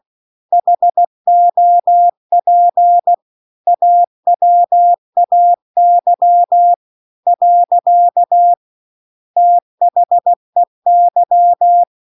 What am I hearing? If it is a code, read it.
Morse code, heard as a keyed tone, about 16 wpm, 700 Hz. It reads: HOP AWAY . THEY
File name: CW-The-Big-Twoheaded-River-2-16WPM